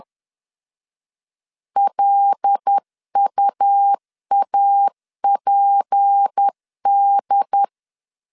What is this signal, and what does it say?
Morse code, heard as a keyed tone, about 10 wpm, 800 Hz. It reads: LUAPD
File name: LEZIONI / 7 GF9Z / 7 G 10wpm 2501